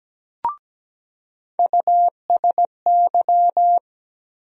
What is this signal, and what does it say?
Morse code, heard as a keyed tone, about 17 wpm, 700 Hz. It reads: USY
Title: USA Call Sign Postfixes - 3 Letters 17wpm